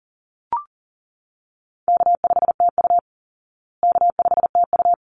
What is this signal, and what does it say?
Morse code, heard as a keyed tone, about 40 wpm, 700 Hz. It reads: K5TU K5TU
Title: Easy Call Signs Repeated 2x - 40wpm